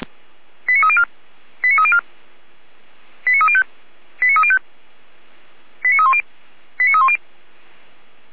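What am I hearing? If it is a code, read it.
Five-tone selective call (ZVEI), heard as quick runs of five tones, tones of about 70 ms. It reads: 89304 89304 89305 89305 89310 89310
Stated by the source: ZVEI five-tone selective call received by SDR.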